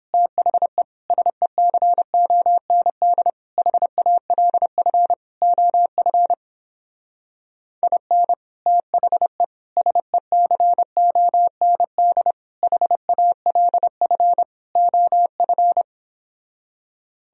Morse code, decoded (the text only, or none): THE SECOND HALF OF IN THE SECOND HALF OF